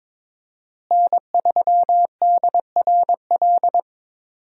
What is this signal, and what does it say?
Morse code, heard as a keyed tone, about 22 wpm, 700 Hz. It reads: N3DRL
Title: USA Call Signs - 5 Characters - Repeated 2x - 22wpm